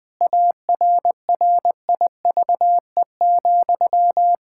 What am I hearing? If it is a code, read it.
Morse code, heard as a keyed tone, about 20 wpm, 700 Hz. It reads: ARRIVE,
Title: Pangram Challenge v2 - 20wpm